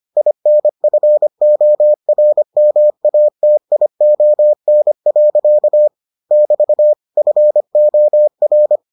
Morse code, heard as a keyed tone, about 25 wpm, 600 Hz. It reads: INFORMATION. <BT> FOR